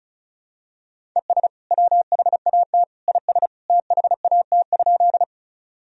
Morse code, heard as a keyed tone, about 36 wpm, 700 Hz. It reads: ES WHAT IS THAT?
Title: Sentences from Top 100 Words and Abbreviations 35wpm